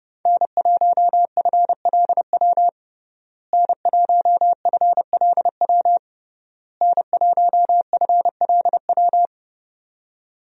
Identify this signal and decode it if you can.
Morse code, heard as a keyed tone, about 30 wpm, 700 Hz. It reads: N1FLW N1FLW N1FLW
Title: USA Call Signs - 5 Characters - Repeated 3x - 30wpm